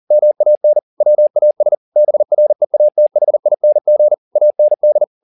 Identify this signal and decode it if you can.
Morse code, heard as a keyed tone, about 40 wpm, 600 Hz. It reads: MAN WAS BREATHING AND
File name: the old man and the sea 2 40wpm